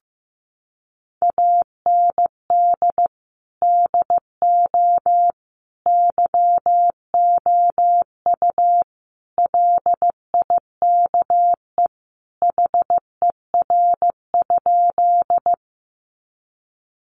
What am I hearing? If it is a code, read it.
Morse code, heard as a keyed tone, about 15 wpm, 700 Hz. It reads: AND DO YOU LIKE HER?
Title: Sentences from Top 100 Words 15wpm